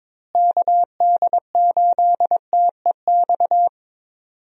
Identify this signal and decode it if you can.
Morse code, heard as a keyed tone, about 22 wpm, 700 Hz. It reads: KD8TEX
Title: USA Call Signs - 6 Characters - Repeated 3x - 22wpm